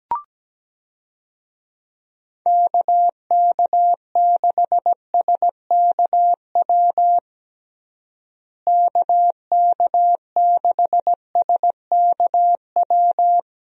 Morse code, heard as a keyed tone, about 17 wpm, 700 Hz. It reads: KK6SKW KK6SKW
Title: USA Call Signs - 6 Characters - Repeated 3x - 17wpm